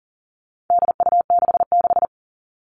Morse code, heard as a keyed tone, about 40 wpm, 700 Hz. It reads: DU66
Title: All Cut Numbers - 4-Digit 40wpm